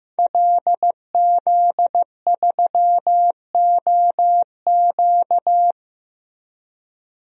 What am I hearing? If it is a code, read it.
Morse code, heard as a keyed tone, about 16 wpm, 700 Hz. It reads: LZ3OQ
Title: DXCC-100 Call Signs - 5 Characters - Repeated 3x - 15wpm